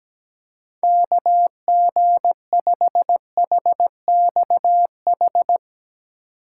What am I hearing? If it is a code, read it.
Morse code, heard as a keyed tone, about 17 wpm, 700 Hz. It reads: KG5HXH
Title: USA Call Signs - 6 Characters - 6x Character Spacing - 17wpm